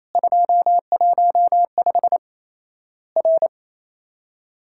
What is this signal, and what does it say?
Morse code, heard as a keyed tone, about 28 wpm, 700 Hz. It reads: 215 R
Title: Worked All Europe DX Contest - QTC Exchange - 28wpm